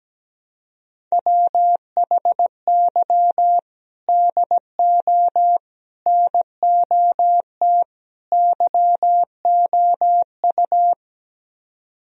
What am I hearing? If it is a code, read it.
Morse code, heard as a keyed tone, about 17 wpm, 700 Hz. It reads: WHY DO NOT YOU
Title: Speed-Racing - Sets of 4 Words 17wpm